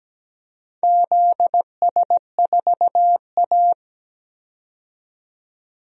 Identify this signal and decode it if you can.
Morse code, heard as a keyed tone, about 17 wpm, 700 Hz. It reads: ZS4A
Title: DXCC-100 Call Signs - 4 Characters - 17wpm